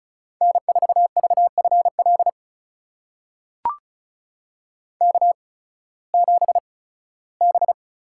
Morse code, heard as a keyed tone, about 36 wpm, spaced as for 19 wpm, 700 Hz. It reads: N4VFL K7B
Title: USA Call Signs - 5 Characters - 8x Character Spacing - 35wpm